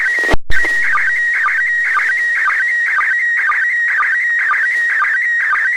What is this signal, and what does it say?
SSTV (slow-scan television), PD 120.
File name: posts / receiving-sstv-images-from-iss / iss sstv2